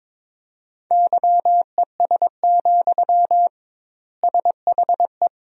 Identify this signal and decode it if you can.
Morse code, heard as a keyed tone, about 22 wpm, 700 Hz. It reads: YES, SHE
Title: 2x Word-Spacing - Sentences from Top 500 Words - 22wpm